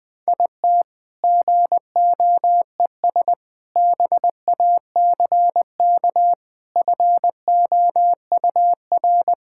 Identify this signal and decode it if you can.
Morse code, heard as a keyed tone, about 20 wpm, 700 Hz. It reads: IT GOES BACK FOUR
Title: All-Age Appropriate Jokes - 20wpm